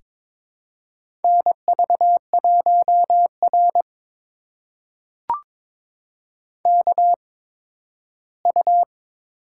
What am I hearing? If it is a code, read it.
Morse code, heard as a keyed tone, about 23 wpm, spaced as for 13 wpm, 700 Hz. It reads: NV1R KU
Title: USA Call Signs - 4 Characters - 8x Character Spacing - 22wpm